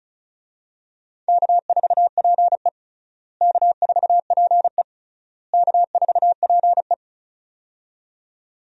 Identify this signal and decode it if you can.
Morse code, heard as a keyed tone, about 34 wpm, 700 Hz. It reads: K4PE K4PE K4PE
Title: USA Call Signs - 4 Characters - Repeated 3x - 35wpm